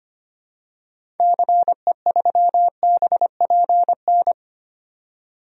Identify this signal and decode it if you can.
Morse code, heard as a keyed tone, about 25 wpm, 700 Hz. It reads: CE3BPN
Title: DXCC-100 Call Signs - 6 Characters - 25wpm